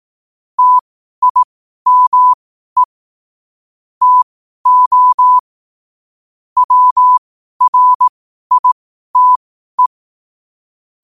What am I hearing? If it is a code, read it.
Morse code, heard as a keyed tone, about 18 wpm, spaced as for 12 wpm, 1000 Hz. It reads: TIME TO WRITE
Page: New quotes every day in morse code at 12 Words per minute.